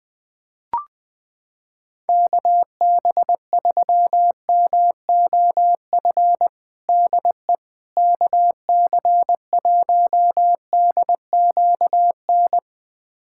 Morse code, heard as a keyed tone, about 20 wpm, 700 Hz. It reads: KB3MOF DE KC1DQN
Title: USA Callsign DE Callsign 20wpm